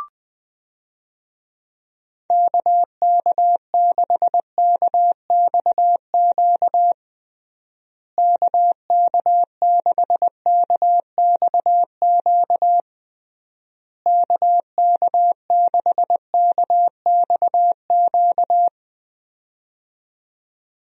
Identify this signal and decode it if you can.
Morse code, heard as a keyed tone, about 20 wpm, 700 Hz. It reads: KK6KXQ KK6KXQ KK6KXQ
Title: USA Call Signs - 6 Characters - Repeated 3x - 20wpm